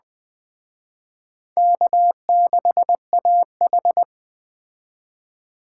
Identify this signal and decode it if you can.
Morse code, heard as a keyed tone, about 20 wpm, 700 Hz. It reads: K6AH
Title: Speed Racing - USA 1x2 Call Signs 17wpm